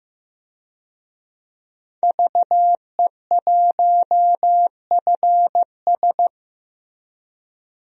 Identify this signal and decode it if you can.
Morse code, heard as a keyed tone, about 15 wpm, 700 Hz. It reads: VE1FS
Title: DXCC-100 Call Signs - 5 Characters - Repeated 2x - 15wpm